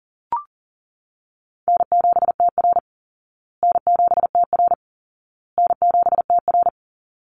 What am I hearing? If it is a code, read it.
Morse code, heard as a keyed tone, about 40 wpm, 700 Hz. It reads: N7TR N7TR N7TR
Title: Call Signs Repeated 2x - 40wpm